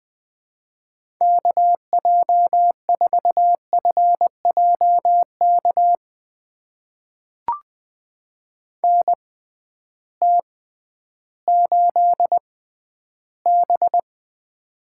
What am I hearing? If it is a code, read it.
Morse code, heard as a keyed tone, about 20 wpm, spaced as for 12 wpm, 700 Hz. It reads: KJ4FJK NT8B
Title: USA Call Signs - 6x Character Spacing - 20wpm